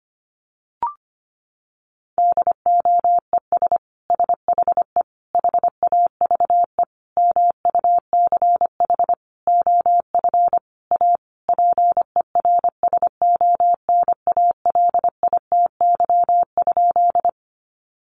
Morse code, heard as a keyed tone, about 25 wpm, 700 Hz. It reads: DOES SHE HAVE MUCH OF A PERSONALITY?
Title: Sentences from Top 1400 Words 25wpm